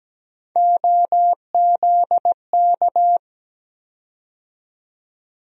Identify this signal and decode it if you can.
Morse code, heard as a keyed tone, about 17 wpm, 700 Hz. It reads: OZK
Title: USA Call Sign Postfixes - 3 Letters 17wpm